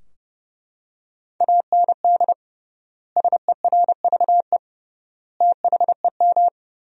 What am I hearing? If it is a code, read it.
Morse code, heard as a keyed tone, about 30 wpm, 700 Hz. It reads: AND SERVE THEM
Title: 3x Word-Spacing - Sentences from Top 400 Words - 30wpm